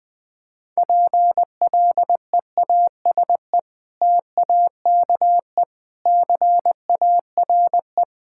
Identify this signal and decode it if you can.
Morse code, heard as a keyed tone, about 20 wpm, 700 Hz. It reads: PLEASE TAKE CARE
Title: Sentences with 3-Word Phrases - 20wpm